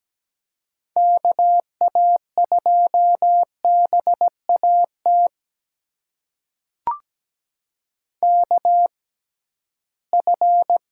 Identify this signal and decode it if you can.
Morse code, heard as a keyed tone, about 17 wpm, spaced as for 13 wpm, 700 Hz. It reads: KA2BAT KF
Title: USA Call Signs - 6 Characters - 6x Character Spacing - 17wpm